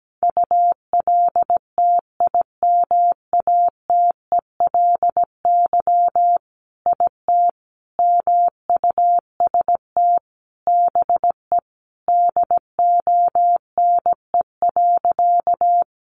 Morse code, heard as a keyed tone, about 17 wpm, 700 Hz. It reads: ULTIMATELY IT MUST BE DONE.
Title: Sentences from Top 1200 Words 17wpm